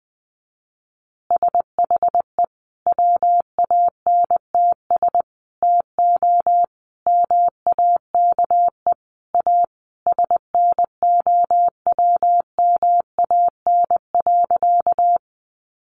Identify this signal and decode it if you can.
Morse code, heard as a keyed tone, about 20 wpm, 700 Hz. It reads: SHE WANTS TO MAKE A SNOWMAN.
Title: Easy Phrases 20wpm